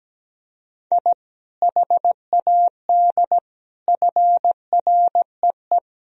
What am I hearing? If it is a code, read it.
Morse code, heard as a keyed tone, about 17 wpm, 700 Hz. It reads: I HAD FREE
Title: Sentences from Top 1000 Words - Review Part 4 - 17wpm